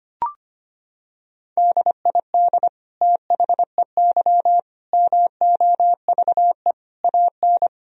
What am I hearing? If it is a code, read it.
Morse code, heard as a keyed tone, about 25 wpm, 700 Hz. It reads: DID THEY MOVE AN
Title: Sentences from Top 1000 Words - Review Part 3 - 25wpm